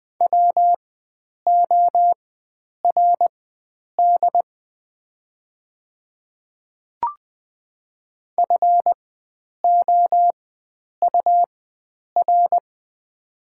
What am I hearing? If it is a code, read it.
Morse code, heard as a keyed tone, about 20 wpm, spaced as for 9 wpm, 690 Hz. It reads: WORD FOUR
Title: 4x Character Spacing - Top 300 Words - 20wpm